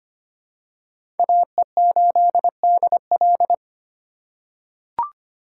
Morse code, heard as a keyed tone, about 25 wpm, 700 Hz. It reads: AE8DL
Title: USA Call Signs - 5 Characters - 3x Character Spacing - 25wpm